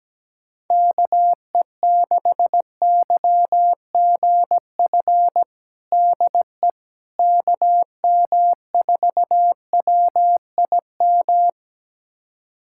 Morse code, heard as a keyed tone, about 17 wpm, 700 Hz. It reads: KE6YGF DE KM4WIM
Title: USA Callsign DE Callsign - 6 Characters - 17wpm